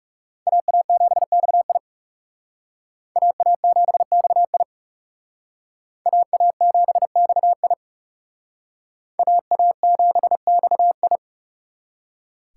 Morse code, heard as a keyed tone, about 38 wpm, 700 Hz. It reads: AA7XI AA7XI AA7XI AA7XI
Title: Speed Racing - USA 2x2 Call Signs 30wpm